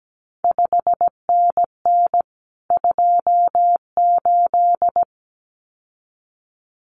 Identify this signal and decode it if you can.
Morse code, heard as a keyed tone, about 17 wpm, 700 Hz. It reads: 5NN 28
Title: CQ WW DX Contest Exchange Encore - 17wpm